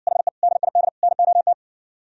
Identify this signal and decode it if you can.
Morse code, heard as a keyed tone, about 60 wpm, 700 Hz. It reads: HE DID NOT